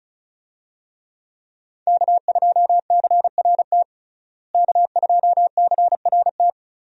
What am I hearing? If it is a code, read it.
Morse code, heard as a keyed tone, about 36 wpm, 700 Hz. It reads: K2CRT K2CRT
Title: USA Call Signs - 5 Characters - Repeated 3x - 35wpm